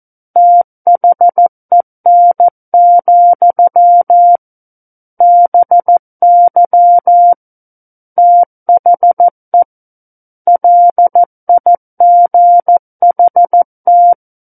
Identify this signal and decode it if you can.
Morse code, heard as a keyed tone, about 14 wpm, 700 Hz. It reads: THEN, BY THE LIGHT
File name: Book One / 05-Chapter 5 - 14 WPM